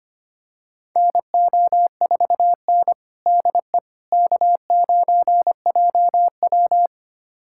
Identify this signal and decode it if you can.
Morse code, heard as a keyed tone, about 25 wpm, 700 Hz. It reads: NO4N DE K9JW
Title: USA Callsign DE Callsign - 4 Characters - 25wpm